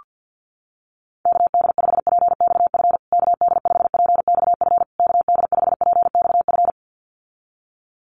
Morse code, heard as a keyed tone, about 48 wpm, 700 Hz. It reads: KD5PXF KD5PXF KD5PXF
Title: USA Call Signs Repeated 3x 50wpm